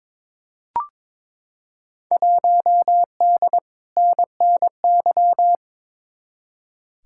Morse code, heard as a keyed tone, about 22 wpm, 700 Hz. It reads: 1D NNY
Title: Field Day Exchanges - 22wpm